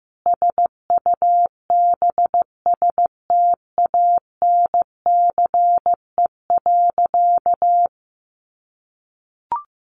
Morse code, heard as a keyed tone, about 14 wpm, 700 Hz. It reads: SUBSTANCE.
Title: Sentences from Top 1500 Words 15wpm